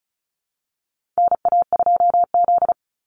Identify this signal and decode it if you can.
Morse code, heard as a keyed tone, about 34 wpm, 700 Hz. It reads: NA2Z
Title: USA Call Signs 35wpm